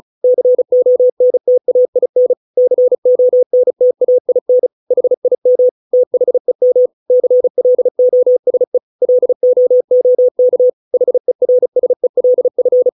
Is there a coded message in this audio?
Yes: Morse code from 0.2 to 12.9 s, heard as a keyed tone, about 34 wpm, 500 Hz. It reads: CONTAIN CONTAIN HIM THEM CLOSE LOOK HERSELF